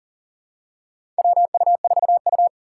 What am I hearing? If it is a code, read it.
Morse code, heard as a keyed tone, about 38 wpm, 700 Hz. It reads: WU4U
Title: USA Call Signs - 6x Character Spacing - 40wpm